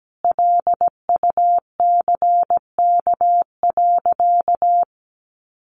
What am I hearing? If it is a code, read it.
Morse code, heard as a keyed tone, about 17 wpm, 700 Hz. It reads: LUCK.
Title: Sentences from Top 1500 Words 17wpm